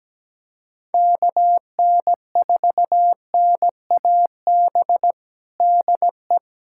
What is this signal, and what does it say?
Morse code, heard as a keyed tone, about 17 wpm, 700 Hz. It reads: KN4NAB DE
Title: USA Callsign DE Callsign - 6 Characters - 17wpm